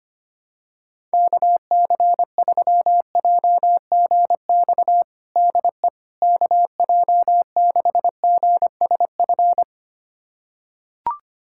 Morse code, heard as a keyed tone, about 25 wpm, 700 Hz. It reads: KC3JGX DE KJ6GSF
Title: USA Callsign DE Callsign - 6 Characters - 25wpm